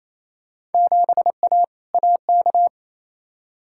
Morse code, heard as a keyed tone, about 28 wpm, 700 Hz. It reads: 7A AK
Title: Field Day Exchanges Encore - 28wpm